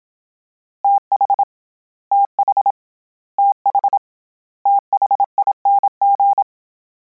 Morse code, heard as a keyed tone, about 27 wpm, 800 Hz. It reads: TH TH TH THING
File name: Combinations / Combinations-mp3s / Combinations@25